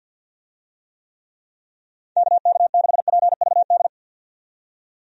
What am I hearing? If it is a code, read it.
Morse code, heard as a keyed tone, about 52 wpm, 700 Hz. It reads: KK6PVD
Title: USA Call Signs - 6 Characters - 50wpm